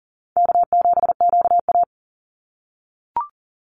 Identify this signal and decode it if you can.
Morse code, heard as a keyed tone, about 40 wpm, 700 Hz. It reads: K7QA
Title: Call Signs - 40wpm